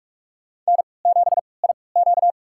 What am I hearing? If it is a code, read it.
Morse code, heard as a keyed tone, about 46 wpm, spaced as for 26 wpm, 700 Hz. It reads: N7IQ
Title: USA Call Signs - 3x Character Spacing - 45wpm